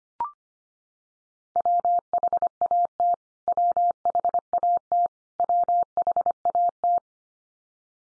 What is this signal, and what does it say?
Morse code, heard as a keyed tone, about 25 wpm, 700 Hz. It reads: WHAT WHAT WHAT
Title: Top 100 Words 25wpm x3